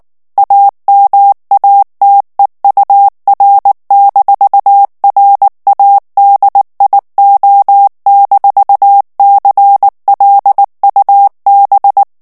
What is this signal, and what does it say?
Morse code, heard as a keyed tone, about 19 wpm, 800 Hz. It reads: AMATEUR-RADIO-CLUB
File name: morse mp3 / ALLEMAND texte clair 2 vitesse 18 mots minute